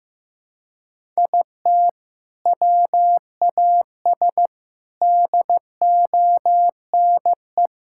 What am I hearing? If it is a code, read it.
Morse code, heard as a keyed tone, about 16 wpm, 700 Hz. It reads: IT WAS DONE
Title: Sentences from Top 1000 Words - Review Part 5 - 15wpm